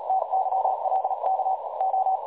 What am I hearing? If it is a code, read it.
Morse code, heard as a keyed tone, about 36 wpm, spaced as for 30 wpm, 760 Hz, with noise. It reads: TYLKO
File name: dlazaawansowanych / nagraniazpasma / websdr recording 3565.0kHz